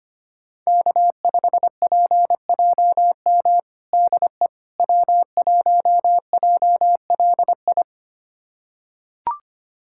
Morse code, heard as a keyed tone, about 25 wpm, 700 Hz. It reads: K5PJM DE W1JLI
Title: USA Callsign DE Callsign - 5 Characters - 25wpm